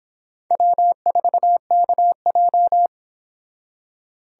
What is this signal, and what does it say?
Morse code, heard as a keyed tone, about 26 wpm, 700 Hz. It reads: W4KJ